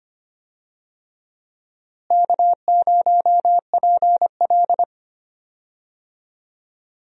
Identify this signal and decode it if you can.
Morse code, heard as a keyed tone, about 25 wpm, 700 Hz. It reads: K0PL
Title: USA Call Signs - 4 Characters - Repeated 3x - 25wpm